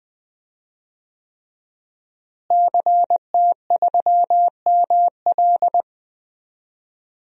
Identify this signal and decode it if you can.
Morse code, heard as a keyed tone, about 20 wpm, 700 Hz. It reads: CT3ML
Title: DXCC-100 Call Signs - 5 Characters - Repeated 3x - 20wpm